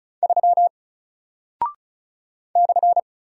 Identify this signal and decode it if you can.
Morse code, heard as a keyed tone, about 36 wpm, 700 Hz. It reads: 3 /
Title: Single Character (Letter:Number:Punctuation:Prosign) 35wpm